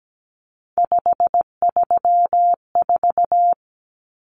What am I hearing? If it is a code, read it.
Morse code, heard as a keyed tone, about 17 wpm, 700 Hz. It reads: 534
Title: Common Cut Numbers - 3-Digit 17wpm